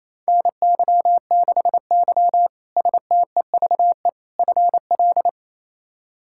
Morse code, heard as a keyed tone, about 28 wpm, 700 Hz. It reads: NY6Y STEVE FL
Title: K1USN SST QSO Practice - 28wpm